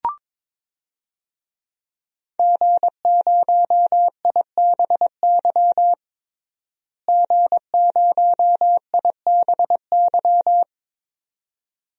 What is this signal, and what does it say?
Morse code, heard as a keyed tone, about 21 wpm, 700 Hz. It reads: G0IBY G0IBY
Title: DXCC-100 Call Signs - 5 Characters - Repeated 2x - 22wpm